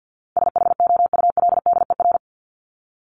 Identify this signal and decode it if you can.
Morse code, heard as a keyed tone, about 50 wpm, 700 Hz. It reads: SHOULDER
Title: Top 1501-1750 Words 50wpm